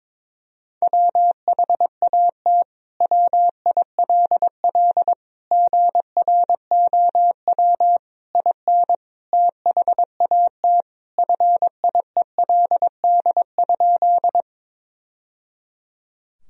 Morse code, heard as a keyed tone, about 22 wpm, 700 Hz. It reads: WHAT WILL GROW IN THAT FIELD?
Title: Sentences from Top 1000 Words - Review Part 2 - 22wpm